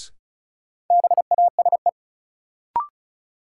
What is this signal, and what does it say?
Morse code, heard as a keyed tone, about 36 wpm, 700 Hz. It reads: BASE
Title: Sentences from Top 1600 Words - New Word Drill Alt - 35wpm